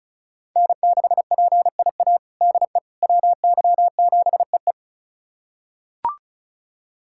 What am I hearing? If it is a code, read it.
Morse code, heard as a keyed tone, about 36 wpm, 700 Hz. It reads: N6PIA DE WY7EE
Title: USA Callsign DE Callsign - 5 Characters - 35wpm